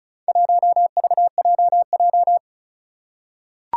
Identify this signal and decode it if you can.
Morse code, heard as a keyed tone, about 36 wpm, 700 Hz. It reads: 1VJJ
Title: Four Challenging Letter-Number Combinations 35wpm